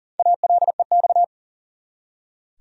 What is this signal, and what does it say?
Morse code, heard as a keyed tone, about 40 wpm, 700 Hz. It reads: ALEX
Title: USA Common Names - Max 5 Letters - 40wpm